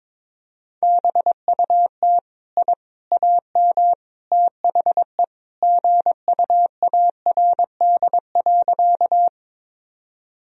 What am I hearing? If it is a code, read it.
Morse code, heard as a keyed tone, about 21 wpm, 700 Hz. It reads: BUT I AM THE GUARD.
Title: Sentences from Top 1000 Words - Review Part 4 - 22wpm